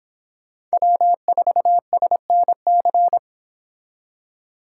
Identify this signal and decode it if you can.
Morse code, heard as a keyed tone, about 30 wpm, 700 Hz. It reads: W4SNC